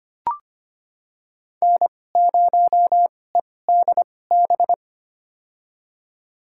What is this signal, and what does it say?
Morse code, heard as a keyed tone, about 25 wpm, spaced as for 18 wpm, 700 Hz. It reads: N0EDB
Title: USA Call Signs - 5 Characters - 2x Character Spacing - 25wpm